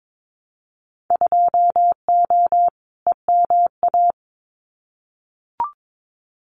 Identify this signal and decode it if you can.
Morse code, heard as a keyed tone, about 22 wpm, 700 Hz. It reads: 2O EMA
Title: Winter Field Day Exchanges 22wpm